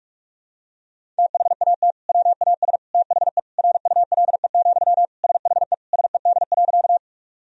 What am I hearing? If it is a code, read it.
Morse code, heard as a keyed tone, about 46 wpm, 700 Hz. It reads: THAT WAS THE RULE, SHE SED.
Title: Sentences from Top 400 Words and Abbreviations 45wpm